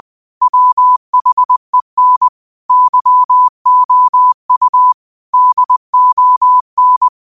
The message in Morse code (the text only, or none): WHEN YOU DON